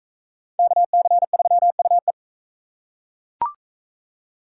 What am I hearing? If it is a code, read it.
Morse code, heard as a keyed tone, about 41 wpm, 700 Hz. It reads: KC3UE